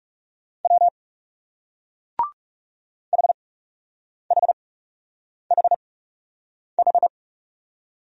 Morse code, heard as a keyed tone, about 38 wpm, spaced as for 8 wpm, 700 Hz. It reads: W HHHH